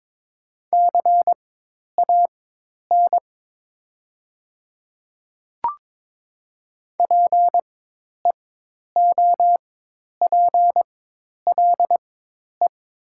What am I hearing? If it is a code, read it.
Morse code, heard as a keyed tone, about 22 wpm, spaced as for 10 wpm, 700 Hz. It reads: CAN PEOPLE
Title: 4x Character Spacing - Top 200 Words - 22wpm